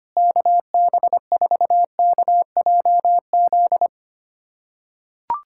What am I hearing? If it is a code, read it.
Morse code, heard as a keyed tone, about 25 wpm, 700 Hz. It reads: KB4KJZ
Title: USA Callsign DE Callsign - 6 Characters - 25wpm